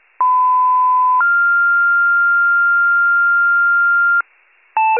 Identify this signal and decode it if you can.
Two-tone page, heard as a one-second tone, then a three-second tone, about 1010 then 1400 Hz. It is heard from 0.2 to 4.2 s.